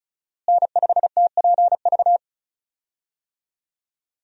Morse code, heard as a keyed tone, about 36 wpm, 700 Hz. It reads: N5TPV
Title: USA Call Signs - 5 Characters - Repeated 2x - 35wpm